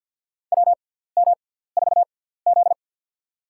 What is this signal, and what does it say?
Morse code, heard as a keyed tone, about 53 wpm, spaced as for 17 wpm, 700 Hz. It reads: WM4Z
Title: USA Call Signs - 6x Character Spacing - 50wpm